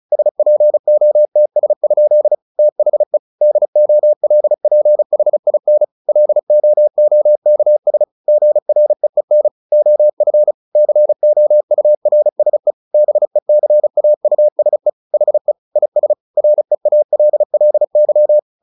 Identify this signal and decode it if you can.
Morse code, heard as a keyed tone, about 34 wpm, 600 Hz. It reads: SPOTS? THE DOLPHIN LOOKS GREEN OF COURSE BECAUSE HE IS REALLY
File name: the old man and the sea 2 35wpm